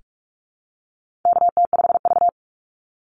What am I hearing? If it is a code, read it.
Morse code, heard as a keyed tone, about 44 wpm, 700 Hz. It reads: KT5V
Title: Call Signs Repeated 2x - 45wpm